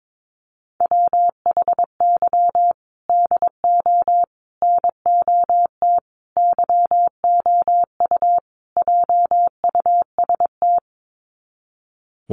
Morse code, heard as a keyed tone, about 22 wpm, 700 Hz. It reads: WHY DO NOT YOU JUST
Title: Sets of 5 Words 22wpm